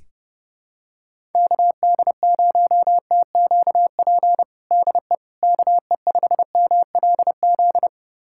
Morse code, heard as a keyed tone, about 32 wpm, 700 Hz. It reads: KD0TQP DE KE5MLZ
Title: USA Callsign DE Callsign - 6 Characters - 30wpm